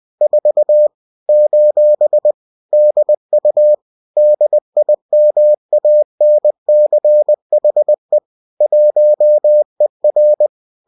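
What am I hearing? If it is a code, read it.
Morse code, heard as a keyed tone, about 20 wpm, 600 Hz. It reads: 4 : DU DIMANCHE 1ER
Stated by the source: Bulletin sent in CW.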